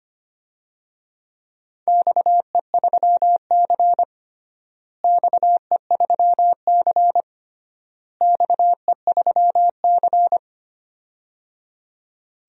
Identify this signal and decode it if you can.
Morse code, heard as a keyed tone, about 25 wpm, 700 Hz. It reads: XE3C XE3C XE3C
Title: DXCC-100 Call Signs - 4 Characters - Repeated 3x - 25wpm